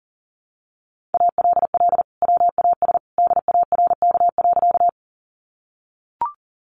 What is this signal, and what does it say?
Morse code, heard as a keyed tone, about 37 wpm, 700 Hz. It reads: ALL WAS DARK.
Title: Sentences from Top 500 Words 40wpm